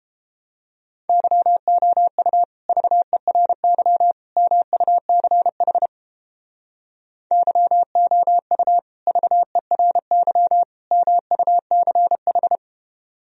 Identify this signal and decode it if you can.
Morse code, heard as a keyed tone, about 33 wpm, 700 Hz. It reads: YOU VERY MUCH YOU VERY MUCH